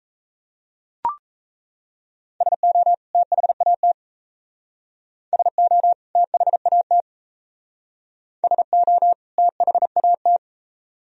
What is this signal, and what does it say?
Morse code, heard as a keyed tone, about 36 wpm, 700 Hz. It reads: SO THAT SO THAT SO THAT